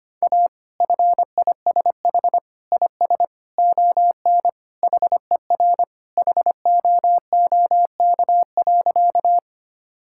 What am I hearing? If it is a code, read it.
Morse code, heard as a keyed tone, about 25 wpm, 700 Hz. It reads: A FISH IS ON HER HOOK.
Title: Sentences from Top 1600 Words 25wpm